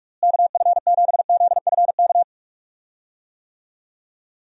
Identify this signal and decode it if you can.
Morse code, heard as a keyed tone, about 44 wpm, 700 Hz. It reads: KF7ZFK
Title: USA Call Signs - 6 Characters - Repeated 3x - 45wpm